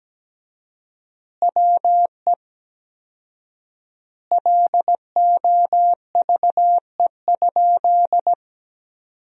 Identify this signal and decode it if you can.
Morse code, heard as a keyed tone, about 17 wpm, 700 Hz. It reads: WE LOVE?
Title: 4x Word-Spacing - Sentences from Top 400 Words - 17wpm